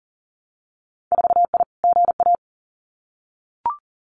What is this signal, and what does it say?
Morse code, heard as a keyed tone, about 41 wpm, 700 Hz. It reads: 4I GA
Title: Winter Field Day Exchanges Encore 40wpm